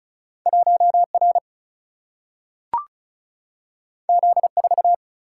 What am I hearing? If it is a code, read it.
Morse code, heard as a keyed tone, about 36 wpm, 700 Hz. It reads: 1R Z4
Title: Single-Letter Single-Number 35wpm